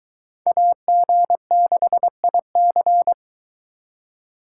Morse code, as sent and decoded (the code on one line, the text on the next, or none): .- --. -.... .. -.-.
AG6IC